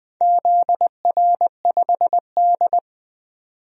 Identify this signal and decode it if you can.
Morse code, heard as a keyed tone, about 21 wpm, 700 Hz. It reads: ZR5D
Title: DXCC-100 Call Signs - 4 Characters - 20wpm